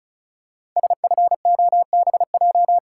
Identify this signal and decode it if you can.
Morse code, heard as a keyed tone, about 36 wpm, 700 Hz. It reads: SFOBJ
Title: Code Groups - 5 Letters - 35wpm